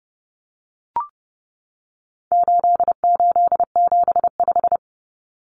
Morse code, heard as a keyed tone, about 30 wpm, 700 Hz. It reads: 8875
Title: Common Cut Numbers - 4-Digit 30wpm